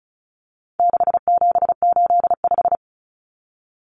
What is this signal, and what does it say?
Morse code, heard as a keyed tone, about 34 wpm, 700 Hz. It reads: 6785
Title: Common Cut Numbers - 4-Digit 35wpm